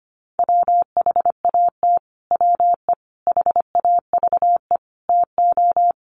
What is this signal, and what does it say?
Morse code, heard as a keyed tone, about 25 wpm, 700 Hz. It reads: WHAT WE HAVE TO
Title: Sets of 5 Words Encore 25wpm